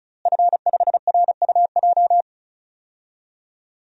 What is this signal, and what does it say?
Morse code, heard as a keyed tone, about 36 wpm, 700 Hz. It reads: F5RUJ
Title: DXCC-100 Call Signs - 5 Characters - Repeated 3x - 35wpm